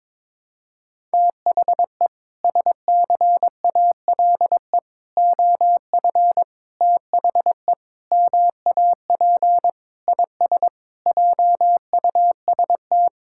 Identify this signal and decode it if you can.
Morse code, heard as a keyed tone, about 22 wpm, 700 Hz. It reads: THE SCALE OF THE MAP IS JUST
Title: Sentences from Top 1000 Words - Review Part 5 - 22wpm